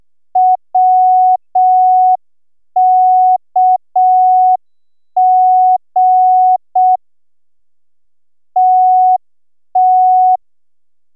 Morse code, heard as a keyed tone, about 6 wpm, 740 Hz. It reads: WKG TT